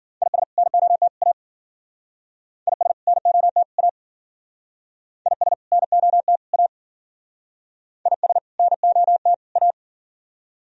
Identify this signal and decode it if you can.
Morse code, heard as a keyed tone, about 51 wpm, 700 Hz. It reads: IS NOT A IS NOT A IS NOT A IS NOT A